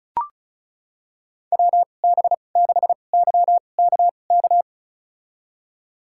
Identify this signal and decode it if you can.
Morse code, heard as a keyed tone, about 36 wpm, spaced as for 25 wpm, 700 Hz. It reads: WB6YKK
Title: USA Call Signs - 6 Characters - 2x Character Spacing - 35wpm